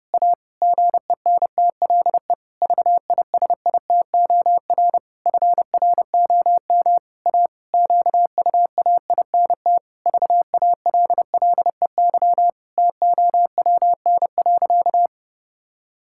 Morse code, heard as a keyed tone, about 32 wpm, 700 Hz. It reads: A GENTLE VISITOR FROM A QUAINT VALLEY TOWN.
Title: Pangram Challenge v2 - 30wpm